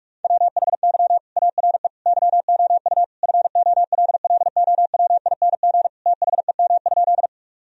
Morse code, heard as a keyed tone, about 46 wpm, 700 Hz. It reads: WHY ARE YOU FOLLOWING THEM?
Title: Sentences from Top 1000 Words - Review Part 4 - 45wpm